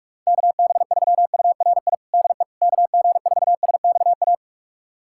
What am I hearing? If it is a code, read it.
Morse code, heard as a keyed tone, about 46 wpm, 700 Hz. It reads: KB3URI DE KG4SXA
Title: USA Callsign DE Callsign 45wpm